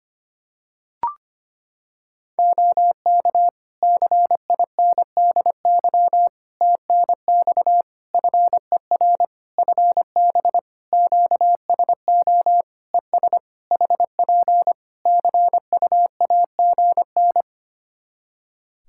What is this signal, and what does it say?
Morse code, heard as a keyed tone, about 25 wpm, 700 Hz. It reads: OK CINDY TNX FER FB QSO ES HP CUAGN
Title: K1USN SST QSO Practice 25wpm